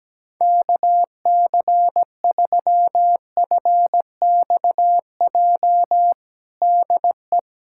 Morse code, heard as a keyed tone, about 17 wpm, 700 Hz. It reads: KC3FXJ DE
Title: USA Callsign DE Callsign 17wpm